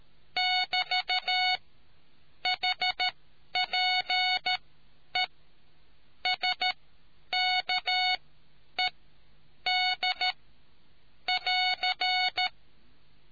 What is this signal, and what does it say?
Morse code, heard as a keyed tone, about 13 wpm, spaced as for 9 wpm, 740 Hz. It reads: <BT> HPE SKED <AR>